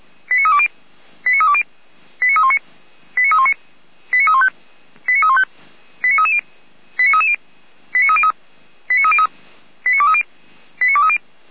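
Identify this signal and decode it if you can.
Five-tone selective call (ZVEI), heard as quick runs of five tones, tones of about 70 ms. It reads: 89320 89320 89319 89319 89315 89315 89330 89330 89303 89303 89230 89230
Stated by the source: ZVEI five-tone selective call received with an SDR.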